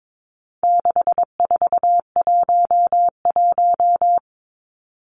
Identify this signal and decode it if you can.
Morse code, heard as a keyed tone, about 22 wpm, 700 Hz. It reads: 6411
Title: Numbers - 4-Digit 22wpm